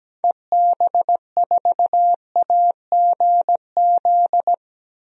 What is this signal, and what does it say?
Morse code, heard as a keyed tone, about 17 wpm, 700 Hz. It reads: EB4AGZ
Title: DXCC-100 Call Signs - 6 Characters - Repeated 3x - 17wpm